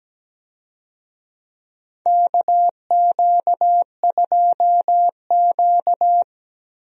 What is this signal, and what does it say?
Morse code, heard as a keyed tone, about 17 wpm, 700 Hz. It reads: KQ2Q
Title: USA Call Signs - 4 Characters - Repeated 2x - 17wpm